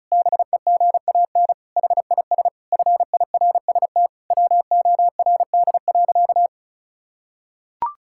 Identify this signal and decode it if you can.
Morse code, heard as a keyed tone, about 36 wpm, 700 Hz. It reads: BEGAN HIS FIRST WORD.
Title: Pangram Challenge v2 - 35wpm